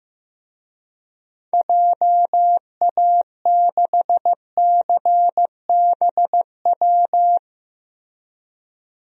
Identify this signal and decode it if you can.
Morse code, heard as a keyed tone, about 15 wpm, 700 Hz. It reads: JA6CBW
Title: DXCC-100 Call Signs - 6 Characters - 15wpm